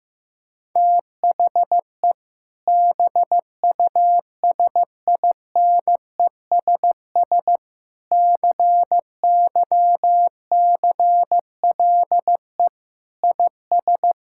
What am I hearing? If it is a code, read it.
Morse code, heard as a keyed tone, about 15 wpm, 700 Hz. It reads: THE BUSINESS CYCLE IS
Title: Sentences from Top 1000 Words - Review Part 6 - 15wpm